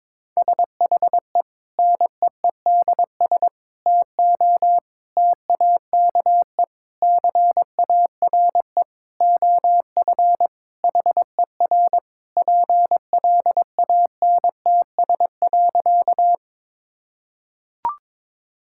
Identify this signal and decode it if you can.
Morse code, heard as a keyed tone, about 22 wpm, 700 Hz. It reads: SHE NEEDS TO TAKE CARE OF HER PLANTS.
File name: practice / 22wpm / Sentences with 4-Word Phrases - 22wpm